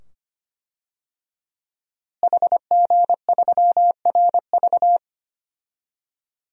Morse code, heard as a keyed tone, about 25 wpm, 700 Hz. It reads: HG3RV
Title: DXCC-100 Call Signs - 5 Characters - Repeated 2x - 25wpm